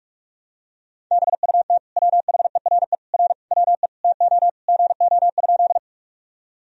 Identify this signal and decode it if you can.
Morse code, heard as a keyed tone, about 46 wpm, 700 Hz. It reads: BUT WHERE R WE TO GO?
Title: Sentences from Top 200 Words and Abbreviations 45wpm